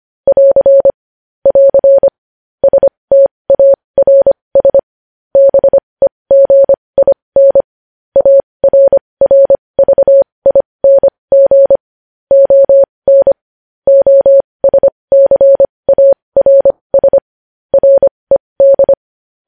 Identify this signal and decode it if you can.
Morse code, heard as a keyed tone, about 25 wpm, spaced as for 20 wpm, 550 Hz. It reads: <AR> <AR> STARS BEGIN ARRVING ON OSCARS RED
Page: news-headlines-in-morse-code-at-20-wpm